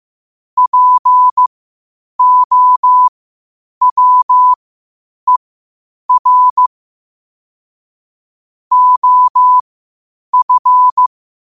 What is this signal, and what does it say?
Morse code, heard as a keyed tone, about 15 wpm, spaced as for 8 wpm, 1000 Hz. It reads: POWER OF